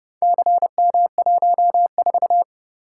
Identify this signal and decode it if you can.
Morse code, heard as a keyed tone, about 28 wpm, 700 Hz. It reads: CM14